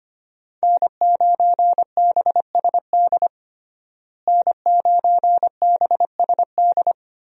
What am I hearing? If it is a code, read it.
Morse code, heard as a keyed tone, about 25 wpm, 700 Hz. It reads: N9BSD N9BSD
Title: USA Call Signs - 5 Characters - Repeated 2x - 25wpm